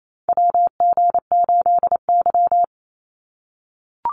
Morse code, heard as a keyed tone, about 28 wpm, 700 Hz. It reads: WG8Y
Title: NA SOTA Chaser Callsigns 28wpm